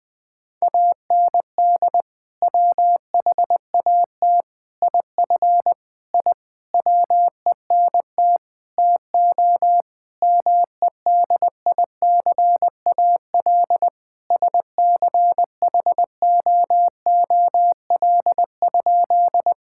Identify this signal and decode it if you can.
Morse code, heard as a keyed tone, about 19 wpm, 700 Hz. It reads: AND WHAT IF I WENT TO MEDICAL SCHOOL?
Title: Sentences from Top 1000 Words - Review Part 5 - 20wpm